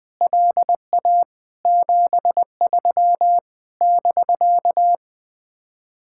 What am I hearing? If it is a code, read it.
Morse code, heard as a keyed tone, about 20 wpm, 700 Hz. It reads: LA 73 <BK>
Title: Speed-Racing Common QSO Phrases - 20wpm